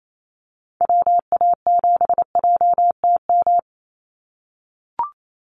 Morse code, heard as a keyed tone, about 28 wpm, 700 Hz. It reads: WA7JTM
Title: NA SOTA Activator Callsigns 28wpm